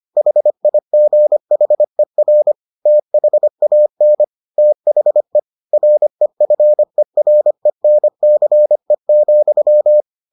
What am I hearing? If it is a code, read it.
Morse code, heard as a keyed tone, about 25 wpm, 600 Hz. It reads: HIGHER THAN THE REFERENCE,